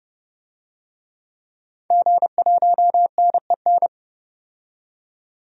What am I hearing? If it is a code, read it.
Morse code, heard as a keyed tone, about 30 wpm, 700 Hz. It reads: G1NEN
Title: DXCC-100 Call Signs - 5 Characters - 30wpm